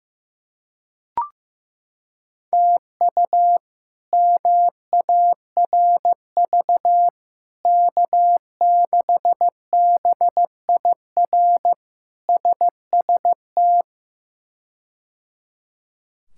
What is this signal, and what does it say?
Morse code, heard as a keyed tone, about 15 wpm, 700 Hz. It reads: TU MARV K6BIR SST
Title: K1USN SST QSO Practice - 15wpm